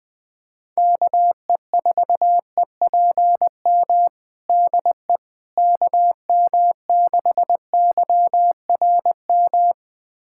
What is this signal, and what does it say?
Morse code, heard as a keyed tone, about 20 wpm, 700 Hz. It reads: KE4EPM DE KM6YRM
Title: USA Callsign DE Callsign - 6 Characters - 20wpm